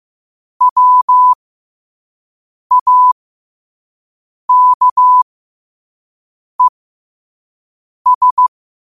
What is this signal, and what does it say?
Morse code, heard as a keyed tone, about 15 wpm, spaced as for 5 wpm, 1000 Hz. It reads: WAKES
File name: qotdfiles / 5wpm